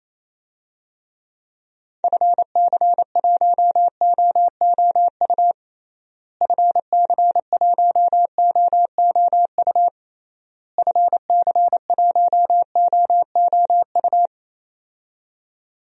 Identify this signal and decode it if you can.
Morse code, heard as a keyed tone, about 28 wpm, 700 Hz. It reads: FC1OOU FC1OOU FC1OOU
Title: DXCC-100 Call Signs - 6 Characters - Repeated 3x - 28wpm